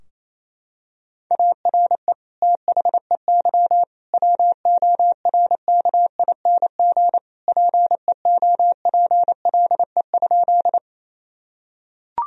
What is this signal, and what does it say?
Morse code, heard as a keyed tone, about 28 wpm, 700 Hz. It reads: ARE THEY WORKING PEOPLE?
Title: Sentences from Top 1000 Words - Review Part 4 - 28wpm